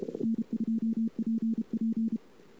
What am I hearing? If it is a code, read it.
Morse code, heard as a keyed tone, about 31 wpm, 220 Hz. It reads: N2PP
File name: CW / 2008-ss-cw-k5tr-op-n5rz -22:00